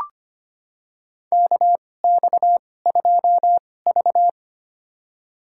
Morse code, heard as a keyed tone, about 25 wpm, spaced as for 18 wpm, 700 Hz. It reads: KX2V
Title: USA Call Signs - 4 Characters - 2x Character Spacing - 25wpm